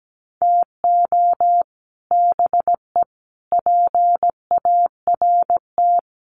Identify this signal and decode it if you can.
Morse code, heard as a keyed tone, about 17 wpm, 700 Hz. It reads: TO BE PART
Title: Sets of 4 Words 17wpm